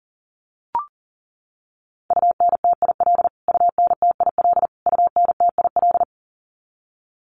Morse code, heard as a keyed tone, about 40 wpm, 700 Hz. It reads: UNTIL UNTIL UNTIL
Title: Top 300 Words Repeated 40wpm